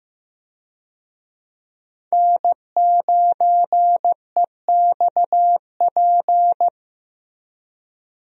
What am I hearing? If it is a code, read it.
Morse code, heard as a keyed tone, about 16 wpm, 700 Hz. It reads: N9EXP
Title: USA Call Signs - 5 Characters - Repeated 3x - 15wpm